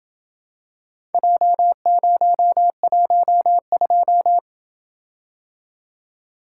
Morse code, heard as a keyed tone, about 28 wpm, 700 Hz. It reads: J012